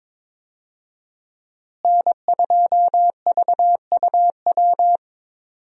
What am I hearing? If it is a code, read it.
Morse code, heard as a keyed tone, about 22 wpm, 700 Hz. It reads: N2VUW
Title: USA Call Signs - 5 Characters - Repeated 3x - 22wpm